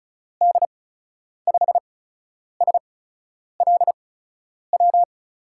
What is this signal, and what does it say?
Morse code, heard as a keyed tone, about 35 wpm, 700 Hz. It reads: D 5 S L W
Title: USA Call Signs - 6 Characters - 8x Character Spacing - 35wpm